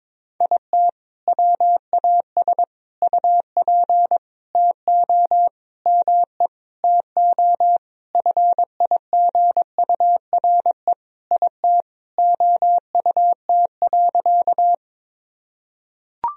Morse code, heard as a keyed tone, about 22 wpm, 700 Hz. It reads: IT WAS UP TO ME TO FIGURE IT OUT.
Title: Sentences from Top 1000 Words - Review Part 2 - 22wpm